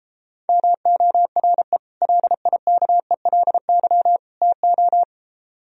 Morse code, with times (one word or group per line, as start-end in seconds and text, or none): 0.5-1.8 MORE
2.0-4.2 LIKELY
4.4-5.0 TO